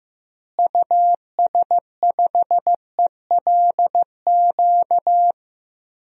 Morse code, heard as a keyed tone, about 15 wpm, 700 Hz. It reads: US5ELQ
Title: DXCC-100 Call Signs - 6 Characters - Repeated 2x - 15wpm